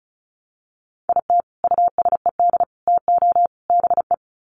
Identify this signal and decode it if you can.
Morse code, heard as a keyed tone, about 34 wpm, 700 Hz. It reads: IT USED TO BE
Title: Sets of 4 Words 35wpm